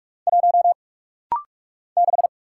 Morse code, heard as a keyed tone, about 44 wpm, 700 Hz. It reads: 1 6
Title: Single Letter-Number - Rapid-Fire 45wpm